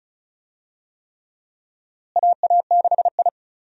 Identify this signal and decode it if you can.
Morse code, heard as a keyed tone, about 34 wpm, 700 Hz. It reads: AA6I
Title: USA Call Signs - 4 Characters - Repeated 3x - 35wpm